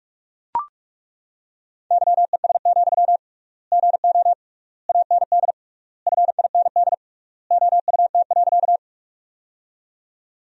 Morse code, heard as a keyed tone, about 41 wpm, 700 Hz. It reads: YES, GO AND FIND OUT.
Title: 3x Word-Spacing - Sentences from Top 500 Words - 45wpm